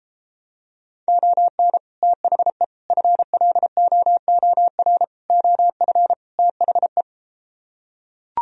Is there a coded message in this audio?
Yes: Morse code from 1.1 to 7.0 s, heard as a keyed tone, about 34 wpm, 700 Hz. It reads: ON THE FLOOR OF THE